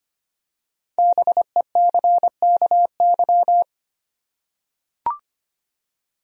Morse code, heard as a keyed tone, about 25 wpm, 700 Hz. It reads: BECKY
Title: USA Common Names - 25wpm